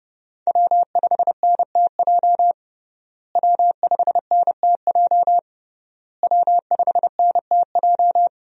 Morse code, heard as a keyed tone, about 31 wpm, 700 Hz. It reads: W5NTJ W5NTJ W5NTJ
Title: USA Call Signs - 5 Characters - Repeated 3x - 30wpm